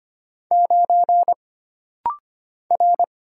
Morse code, heard as a keyed tone, about 25 wpm, 700 Hz. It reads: 9 R
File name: Single Character (Letter:Number:Punctuation:Prosign) - Rapid-Fire - 25wpm